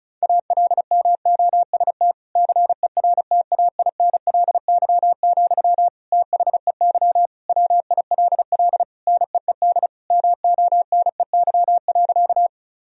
Morse code, heard as a keyed tone, about 34 wpm, 700 Hz. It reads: ALMOST CERTAINLY, THEY WILL NEED MONEY.
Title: Sentences from Top 1000 Words - Review Part 4 - 35wpm